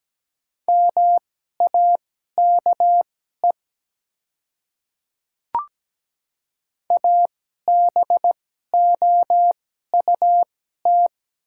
Morse code, heard as a keyed tone, about 17 wpm, spaced as for 12 wpm, 700 Hz. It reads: MAKE ABOUT
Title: 2x Character Spacing - Top 200 Words - 17wpm